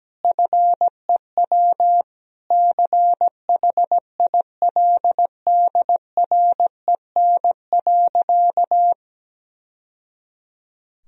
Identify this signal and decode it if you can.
Morse code, heard as a keyed tone, about 17 wpm, 700 Hz. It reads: FEW CHILDREN.
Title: Sentences from Top 1000 Words - Review Part 1 - 17wpm